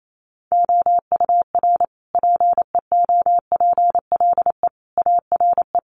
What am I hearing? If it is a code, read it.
Morse code, heard as a keyed tone, about 28 wpm, 700 Hz. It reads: OUR PEOPLE ARE
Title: Sentences from Top 1500 Words 28wpm